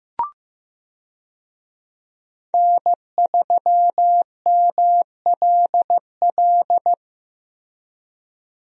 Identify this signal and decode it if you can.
Morse code, heard as a keyed tone, about 15 wpm, 700 Hz. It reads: N3MLL
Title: USA Call Signs - 5 Characters - Repeated 3x - 15wpm